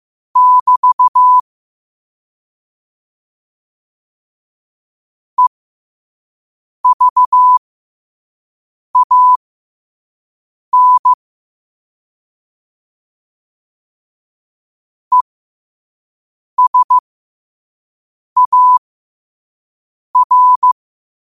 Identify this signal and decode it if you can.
Morse code, heard as a keyed tone, about 15 wpm, spaced as for 5 wpm, 1000 Hz. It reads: <BT> EVAN ESAR
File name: qotdfiles / 5wpm